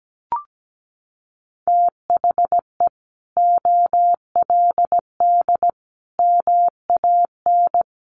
Morse code, heard as a keyed tone, about 17 wpm, 700 Hz. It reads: THE OLD MAN
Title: Sentences from Top 700 Words 17wpm